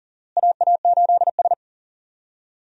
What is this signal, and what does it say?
Morse code, heard as a keyed tone, about 40 wpm, 700 Hz. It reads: AA8S
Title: USA Call Signs - 4 Characters - Repeated 3x - 40wpm